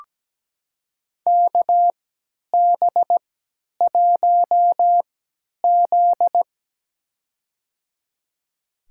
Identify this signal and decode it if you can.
Morse code, heard as a keyed tone, about 17 wpm, spaced as for 10 wpm, 700 Hz. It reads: KB1Z
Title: USA Call Signs - 4 Characters - 3x Character Spacing - 17wpm